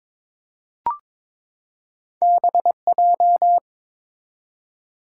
Morse code, heard as a keyed tone, about 22 wpm, 700 Hz. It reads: BJ
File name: practice / 22wpm / Two Letter-Number Combinations - 22wpm